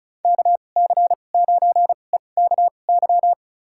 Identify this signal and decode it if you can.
Morse code, heard as a keyed tone, about 36 wpm, spaced as for 25 wpm, 700 Hz. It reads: KC9EKY
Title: USA Call Signs - 6 Characters - 2x Character Spacing - 35wpm